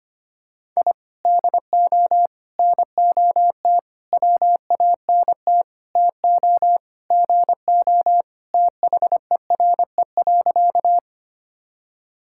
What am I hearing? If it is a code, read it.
Morse code, heard as a keyed tone, about 25 wpm, 700 Hz. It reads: I DO NOT WANT TO GO THERE.
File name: practice / 25wpm / Sentences with 4-Word Phrases - 25wpm